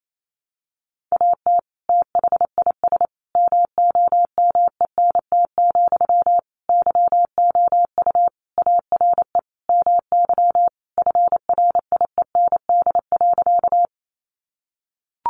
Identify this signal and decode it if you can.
Morse code, heard as a keyed tone, about 28 wpm, 700 Hz. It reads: AT THIS MOMENT, YOU ARE MY FRIEND.
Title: Sentences from Top 1000 Words 28wpm